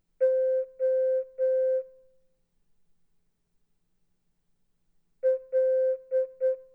Morse code, heard as a keyed tone, about 8 wpm, 520 Hz. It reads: O L
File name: Morse-Gul-oppgave